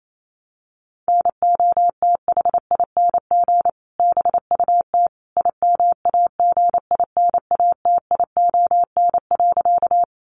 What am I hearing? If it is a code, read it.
Morse code, heard as a keyed tone, about 28 wpm, 700 Hz. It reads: NOTHING BUT IMAGINATION.
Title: Sentences from Top 1300 Words 28wpm